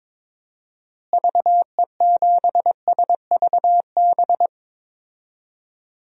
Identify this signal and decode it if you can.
Morse code, heard as a keyed tone, about 22 wpm, 700 Hz. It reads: VE7SVB
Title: DXCC-100 Call Signs - 6 Characters - Repeated 2x - 22wpm